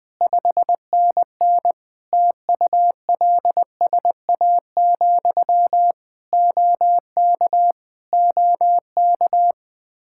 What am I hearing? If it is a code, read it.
Morse code, heard as a keyed tone, about 21 wpm, 700 Hz. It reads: 5NN TULSA, OK OK
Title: Practice QSO - Exchange 1 - Beginning - 20wpm